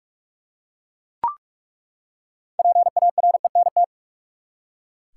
Morse code, heard as a keyed tone, about 42 wpm, 700 Hz. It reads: PARENT
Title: Six Letter Words - 45wpm